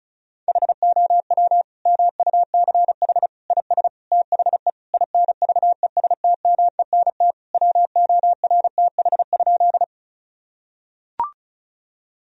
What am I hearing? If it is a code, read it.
Morse code, heard as a keyed tone, about 36 wpm, 700 Hz. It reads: HOW MUCH IS THE INVESTMENT WORTH?
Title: Sentences from Top 1000 Words - Review Part 4 - 35wpm